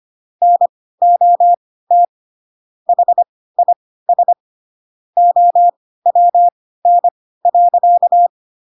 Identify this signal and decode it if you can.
Morse code, heard as a keyed tone, about 25 wpm, spaced as for 16 wpm, 700 Hz. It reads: NOT HIS OWN.
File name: Pinocchio / Pinocchio ch6 10 25 ik1hge